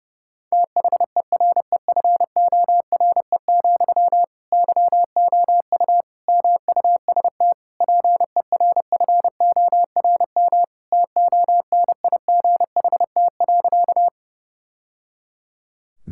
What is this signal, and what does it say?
Morse code, heard as a keyed tone, about 29 wpm, 700 Hz. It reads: THEREFORE, YOU MUST PERFORM TONIGHT.
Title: Sentences from Top 1000 Words - Review Part 5 - 30wpm